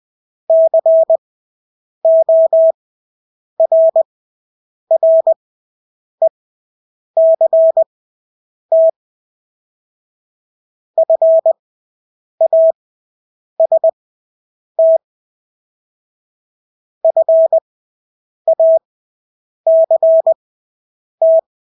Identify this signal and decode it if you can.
Morse code, heard as a keyed tone, about 20 wpm, spaced as for 8 wpm, 650 Hz. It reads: CORRECT FAST FACT
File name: Engelske ord 20/8 wpm